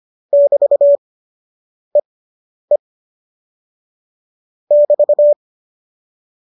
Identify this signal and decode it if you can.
Morse code, heard as a keyed tone, about 25 wpm, spaced as for 9 wpm, 570 Hz. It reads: <BT>EE <BT>